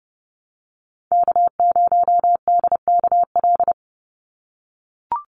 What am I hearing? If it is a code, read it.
Morse code, heard as a keyed tone, about 30 wpm, 700 Hz. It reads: K0DKL
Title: USA Call Signs 30wpm